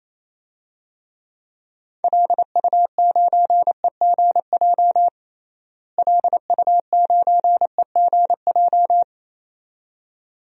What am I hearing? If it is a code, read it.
Morse code, heard as a keyed tone, about 27 wpm, 700 Hz. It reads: LU9EGJ LU9EGJ
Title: DXCC-100 Call Signs - 6 Characters - Repeated 2x - 28wpm